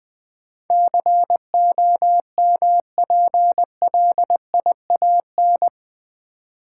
Morse code, heard as a keyed tone, about 20 wpm, 700 Hz. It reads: COMPLIAN
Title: Build A Word - 4 Letters At A Time - 10-Letter Words - 20wpm